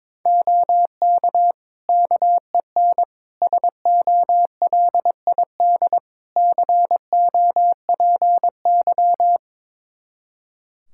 Morse code, heard as a keyed tone, about 22 wpm, 700 Hz. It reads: OK KEN SOLID COPY
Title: Practice QSO - Exchange 3 - More Info - 22wpm